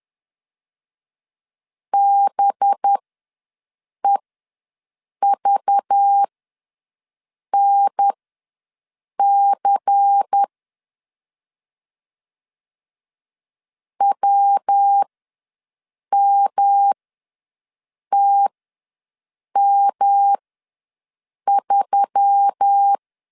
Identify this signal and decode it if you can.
Morse code, heard as a keyed tone, about 10 wpm, spaced as for 6 wpm, 800 Hz. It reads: BEVNC WMTM3
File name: LEZIONI / 7 GF9Z / 7 GF ws 10wpm 3918